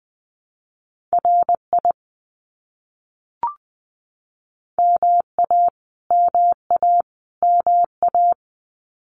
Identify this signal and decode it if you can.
Morse code, heard as a keyed tone, about 20 wpm, 700 Hz. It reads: RI MA MA MA
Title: US States Repeated 3x 20wpm